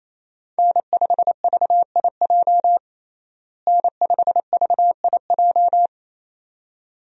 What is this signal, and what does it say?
Morse code, heard as a keyed tone, about 28 wpm, 700 Hz. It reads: N5VIJ N5VIJ
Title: USA Call Signs - 5 Characters - Repeated 2x - 28wpm